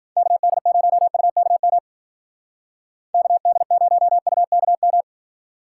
Morse code, heard as a keyed tone, about 52 wpm, 700 Hz. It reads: KD0UKM KD0UKM